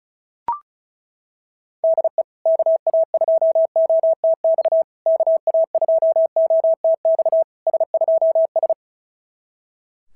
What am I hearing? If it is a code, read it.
Morse code, heard as a keyed tone, about 36 wpm, 660 Hz. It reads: DE KA2OTX KA2OTX S2S
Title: SOTA QSO Practice 35wpm